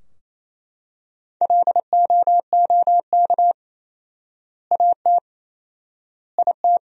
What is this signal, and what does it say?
Morse code, heard as a keyed tone, about 28 wpm, 700 Hz. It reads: LOOK AT IT
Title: 4x Word-Spacing - Sentences from Top 200 Words - 28wpm